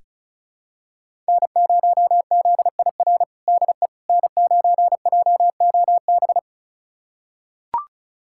Morse code, heard as a keyed tone, about 36 wpm, 700 Hz. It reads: N0ZIR DE N9JOB
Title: USA Callsign DE Callsign - 5 Characters - 35wpm